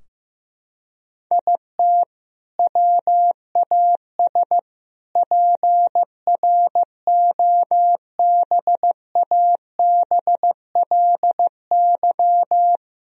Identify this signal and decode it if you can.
Morse code, heard as a keyed tone, about 15 wpm, 700 Hz. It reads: IT WAS PROBABLY
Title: Sentences from Top 1000 Words - Review Part 4 - 15wpm